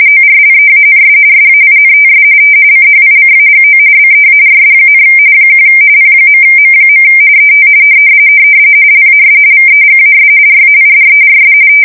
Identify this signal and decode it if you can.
RTTY, about 45.45 baud. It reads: ORLEANS SAINTS CHICAGO BEARS, 1 P.M. ET LAS VEGAS RAIDERS
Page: 170 Shift 2125/2295